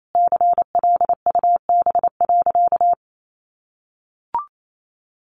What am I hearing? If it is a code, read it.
Morse code, heard as a keyed tone, about 28 wpm, 700 Hz. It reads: CLUB.
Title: Sentences from Top 1200 Words 28wpm